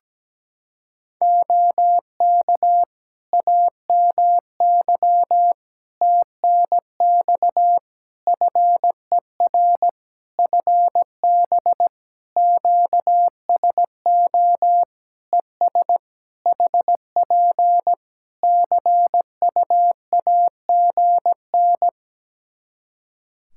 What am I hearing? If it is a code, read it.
Morse code, heard as a keyed tone, about 17 wpm, 700 Hz. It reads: OK AMY TNX FER FB QSO ES HP CUAGN
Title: Practice QSO - Exchange 4 - Ending - 17wpm